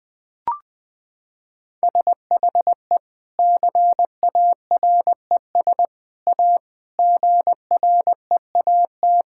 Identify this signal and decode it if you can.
Morse code, heard as a keyed tone, about 20 wpm, 700 Hz. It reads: SHE CARES A GREAT
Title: Sentences with 3-Word Phrases - 20wpm